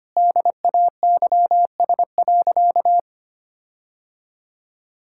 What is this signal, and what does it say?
Morse code, heard as a keyed tone, about 25 wpm, 700 Hz. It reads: DAYS.
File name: practice / 25wpm / Sentences with 4-Word Phrases - 25wpm